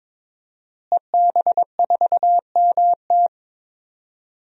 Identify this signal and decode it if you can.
Morse code, heard as a keyed tone, about 22 wpm, 700 Hz. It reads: EB4MT
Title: DXCC-100 Call Signs - 5 Characters - Repeated 3x - 22wpm